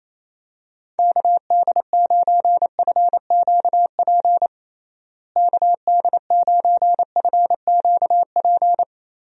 Morse code, heard as a keyed tone, about 28 wpm, 700 Hz. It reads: KD9FQP KD9FQP
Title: USA Call Signs - 6 Characters - Repeated 2x - 28wpm